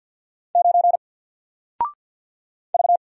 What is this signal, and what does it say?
Morse code, heard as a keyed tone, about 50 wpm, 700 Hz. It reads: 9 V
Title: Single Letter-Number Alt 50wpm